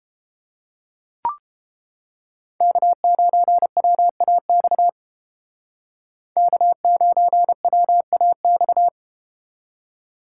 Morse code, heard as a keyed tone, about 30 wpm, 700 Hz. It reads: K9WAX K9WAX